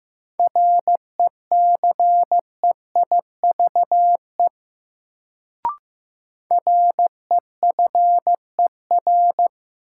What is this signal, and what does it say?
Morse code, heard as a keyed tone, about 16 wpm, 700 Hz. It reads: RECEIVE REFER
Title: Sentences from Top 1000 Words - New Word Drill - 15wpm